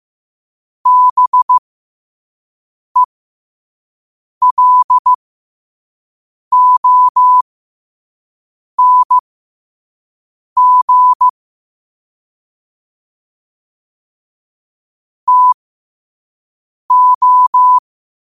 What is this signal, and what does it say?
Morse code, heard as a keyed tone, about 15 wpm, spaced as for 5 wpm, 1000 Hz. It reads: BELONG TO
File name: qotdfiles / 5wpm